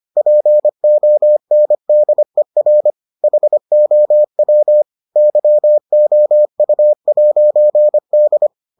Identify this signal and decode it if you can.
Morse code, heard as a keyed tone, about 25 wpm, 600 Hz. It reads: PONDER HOW YOU'D